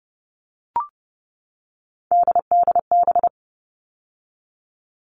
Morse code, heard as a keyed tone, about 30 wpm, 700 Hz. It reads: DDB
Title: All Cut Numbers - 3-Digit 30wpm